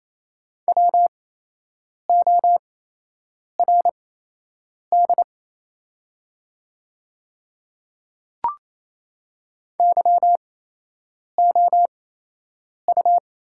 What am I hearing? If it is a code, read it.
Morse code, heard as a keyed tone, about 27 wpm, spaced as for 8 wpm, 700 Hz. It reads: WORD YOU
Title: 8x Character Spacing - Top 100 Words - 28wpm